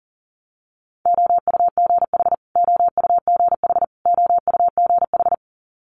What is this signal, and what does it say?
Morse code, heard as a keyed tone, about 42 wpm, 700 Hz. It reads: OUGH OUGH OUGH
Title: Multi-letter Phonemes Repeated 3x 40wpm